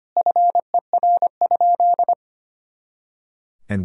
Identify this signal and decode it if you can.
Morse code, heard as a keyed tone, about 25 wpm, 700 Hz. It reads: FER?
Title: Sentences from Top 100 Words and Abbreviations 25wpm